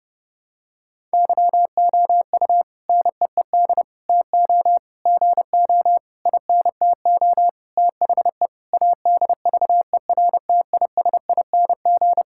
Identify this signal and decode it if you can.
Morse code, heard as a keyed tone, about 30 wpm, 700 Hz. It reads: YOU NEED TO GO INTO THE ADVERTISING
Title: Sentences from Top 1000 Words - Review Part 6 - 30wpm